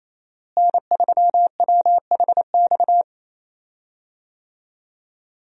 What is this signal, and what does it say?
Morse code, heard as a keyed tone, about 27 wpm, 700 Hz. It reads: N3WHX
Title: USA Call Signs - 5 Characters - Repeated 3x - 28wpm